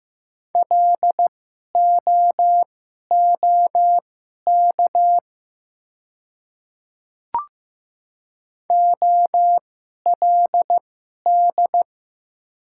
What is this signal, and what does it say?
Morse code, heard as a keyed tone, about 15 wpm, spaced as for 11 wpm, 700 Hz. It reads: LOOK OLD
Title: 2x Character Spacing - Top 200 Words - 15wpm